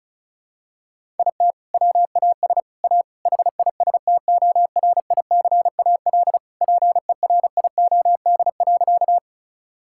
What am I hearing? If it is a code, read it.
Morse code, heard as a keyed tone, about 36 wpm, 700 Hz. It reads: IT WAS A HISTORICAL PERIOD.
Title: Sentences from Top 1000 Words - Review Part 5 - 35wpm